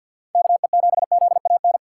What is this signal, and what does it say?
Morse code, heard as a keyed tone, about 46 wpm, 700 Hz. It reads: KE7ZAN
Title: USA Call Signs - 6 Characters - 4x Character Spacing - 50wpm